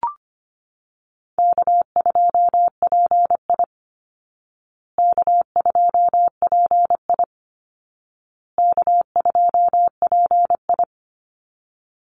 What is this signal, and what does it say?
Morse code, heard as a keyed tone, about 25 wpm, 700 Hz. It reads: K2PI K2PI K2PI
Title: Easy Call Signs Repeated 3x - 25wpm